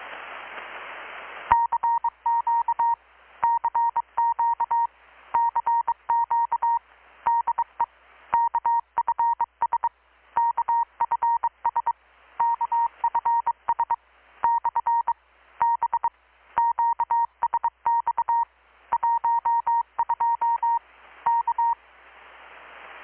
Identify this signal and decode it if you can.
Morse code, heard as a keyed tone, about 24 wpm, 930 Hz. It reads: CQ CQ CQ DE KFS KFS KFS / B QSX 12 K
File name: blog / swl / 6-abril-2019-12695-USB-KPS